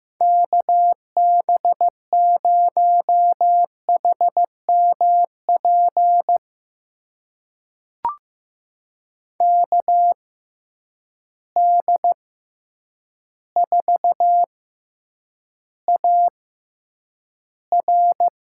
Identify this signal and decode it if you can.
Morse code, heard as a keyed tone, about 15 wpm, spaced as for 8 wpm, 700 Hz. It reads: KB0HMP KD4AR
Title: USA Call Signs - 6 Characters - 6x Character Spacing - 15wpm